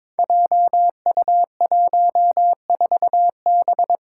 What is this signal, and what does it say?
Morse code, heard as a keyed tone, about 22 wpm, 700 Hz. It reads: JU14B
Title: Five Challenging Letter-Number Combinations 22wpm